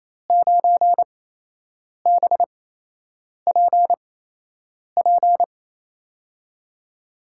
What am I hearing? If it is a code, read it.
Morse code, heard as a keyed tone, about 27 wpm, spaced as for 8 wpm, 700 Hz. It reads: 9BPP
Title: USA Call Signs - 6 Characters - 8x Character Spacing - 28wpm